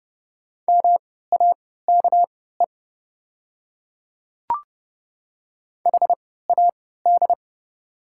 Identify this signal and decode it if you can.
Morse code, heard as a keyed tone, about 30 wpm, spaced as for 17 wpm, 700 Hz. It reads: MAKE HAD
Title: 3x Character Spacing - Top 100 Words - 30wpm